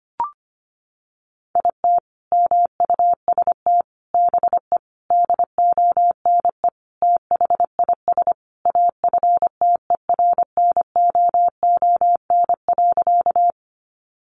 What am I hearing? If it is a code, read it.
Morse code, heard as a keyed tone, about 25 wpm, 700 Hz. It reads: IT MUST BE DONE THIS AFTERNOON.
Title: Sentences from Top 1500 Words 25wpm